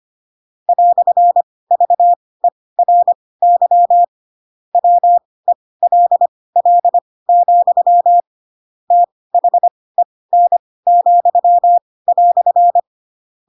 Morse code, heard as a keyed tone, about 25 wpm, spaced as for 18 wpm, 700 Hz. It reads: "VERY WELL, THEN,"
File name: Pinocchio / Pinocchio ch7 12 25 ik1hge